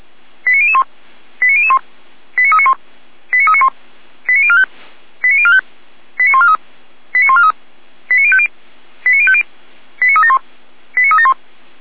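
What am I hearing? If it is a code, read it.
Five-tone selective call (ZVEI), heard as quick runs of five tones, tones of about 70 ms. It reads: 89001 89001 89391 89391 89045 89045 89143 89143 89050 89050 89371 89371
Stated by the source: ZVEI five-tone selective call received by SDR.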